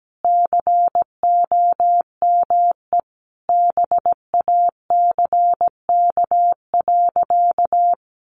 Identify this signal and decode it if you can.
Morse code, heard as a keyed tone, about 17 wpm, 700 Hz. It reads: COME BACK.
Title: A Sound Of Thunder 17wpm